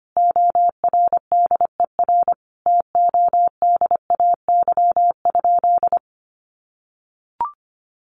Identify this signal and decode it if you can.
Morse code, heard as a keyed tone, about 25 wpm, 700 Hz. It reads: ORDER TODAY?
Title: Sentences from Top 1500 Words 25wpm